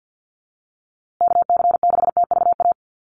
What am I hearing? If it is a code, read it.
Morse code, heard as a keyed tone, about 50 wpm, 700 Hz. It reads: KC6TVA
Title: USA Call Signs Repeated 3x 50wpm